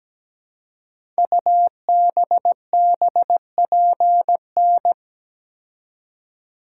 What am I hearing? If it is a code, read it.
Morse code, heard as a keyed tone, about 17 wpm, 700 Hz. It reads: UBBPN
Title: Code Groups - 5 Letter:Numbers - 17wpm